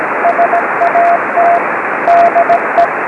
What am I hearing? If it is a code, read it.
Morse code, heard as a keyed tone, about 17 wpm, 690 Hz, with radio noise. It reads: SAT DE
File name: ao40 cw qso